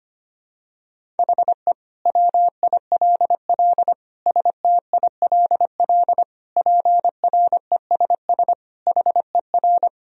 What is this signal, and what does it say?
Morse code, heard as a keyed tone, about 25 wpm, 700 Hz. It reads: HE WILL STILL PRESS HER
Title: Sentences from Top 1000 Words - Review Part 1 - 25wpm